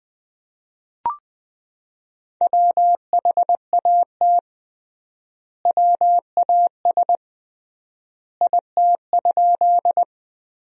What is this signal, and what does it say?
Morse code, heard as a keyed tone, about 20 wpm, 700 Hz. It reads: WHAT WAS IT?
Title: 3x Word-Spacing - Sentences from Top 100 Words - 20wpm